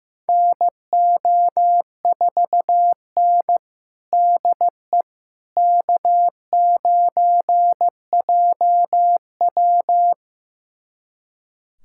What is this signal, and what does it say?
Morse code, heard as a keyed tone, about 15 wpm, 700 Hz. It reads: NO4N DE K9JW
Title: USA Callsign DE Callsign - 4 Characters - 15wpm